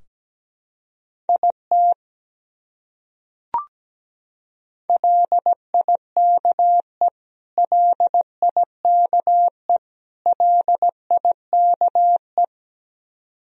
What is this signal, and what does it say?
Morse code, heard as a keyed tone, about 17 wpm, 700 Hz. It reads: IT LIKE LIKE LIKE
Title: Top 100 Words Repeated 17wpm